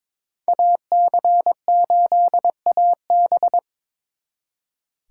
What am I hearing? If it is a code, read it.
Morse code, heard as a keyed tone, about 26 wpm, 700 Hz. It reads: AC8AB
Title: Speed Racing - USA Call Signs 22wpm